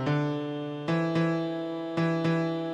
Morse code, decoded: none